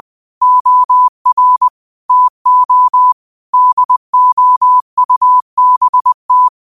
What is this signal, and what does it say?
Morse code, heard as a keyed tone, about 20 wpm, 1000 Hz. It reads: OR TO DOUBT
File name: qotdfiles / 20wpm